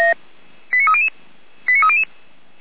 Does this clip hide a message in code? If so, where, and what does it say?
Yes: five-tone selective call (ZVEI) from 0.7 to 2.0 s, heard as quick runs of five tones, tones of about 70 ms. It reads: 89300 89300